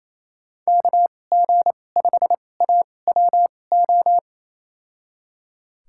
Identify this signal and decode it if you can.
Morse code, heard as a keyed tone, about 28 wpm, spaced as for 20 wpm, 700 Hz. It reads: KG5AWO
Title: USA Call Signs - 6 Characters - 2x Character Spacing - 28wpm